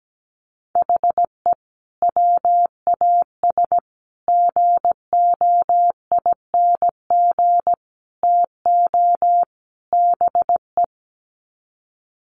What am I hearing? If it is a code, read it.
Morse code, heard as a keyed tone, about 17 wpm, 700 Hz. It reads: HE WAS GOING TO BE
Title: Sets of 5 Words 17wpm